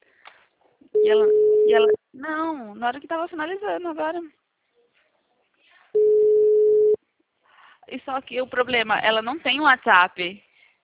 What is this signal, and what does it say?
Ringback tone (Europe, 425 Hz), with speech.
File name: out- -2016- - - .19182